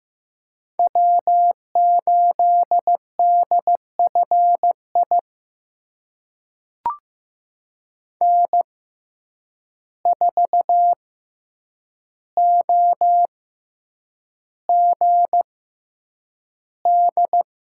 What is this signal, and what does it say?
Morse code, heard as a keyed tone, about 15 wpm, spaced as for 8 wpm, 700 Hz. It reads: W8DFI N4OGD
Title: USA Call Signs - 5 Characters - 6x Character Spacing - 15wpm